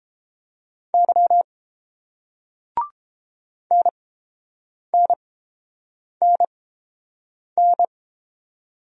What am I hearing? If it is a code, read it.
Morse code, heard as a keyed tone, about 29 wpm, spaced as for 7 wpm, 700 Hz. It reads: Y NNNN